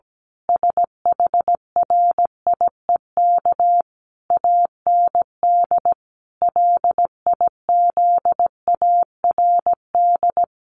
Morse code, heard as a keyed tone, about 17 wpm, 700 Hz. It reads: SHRIEK AND LIZARD
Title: A Sound Of Thunder 17wpm